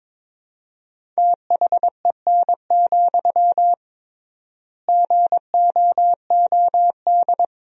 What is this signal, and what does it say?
Morse code, heard as a keyed tone, about 22 wpm, 700 Hz. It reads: THEN, GOOD
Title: 3x Word-Spacing - Sentences from Top 300 Words - 22wpm